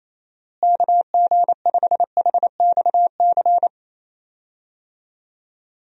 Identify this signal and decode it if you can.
Morse code, heard as a keyed tone, about 28 wpm, 700 Hz. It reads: KG5HXC
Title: USA Call Signs - 6 Characters - Repeated 3x - 28wpm